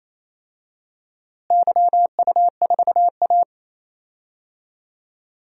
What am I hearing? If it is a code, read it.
Morse code, heard as a keyed tone, about 27 wpm, 700 Hz. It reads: YU4A
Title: DXCC-100 Call Signs - 4 Characters - Repeated 3x - 28wpm